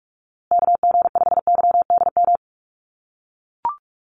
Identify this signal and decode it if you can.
Morse code, heard as a keyed tone, about 44 wpm, 700 Hz. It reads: KG5YDM
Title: USA Call Signs Repeated 3x 45wpm